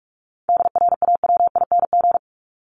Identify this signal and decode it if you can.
Morse code, heard as a keyed tone, about 44 wpm, 700 Hz. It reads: DRAWING
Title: Top 1751-2000 Words 45wpm